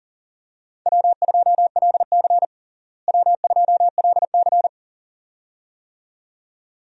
Morse code, heard as a keyed tone, about 40 wpm, 700 Hz. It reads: W2LC W2LC
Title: USA Call Signs - 4 Characters - Repeated 2x - 40wpm